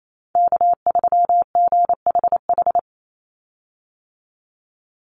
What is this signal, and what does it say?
Morse code, heard as a keyed tone, about 28 wpm, 700 Hz. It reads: K3GHH
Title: Call Signs - 28wpm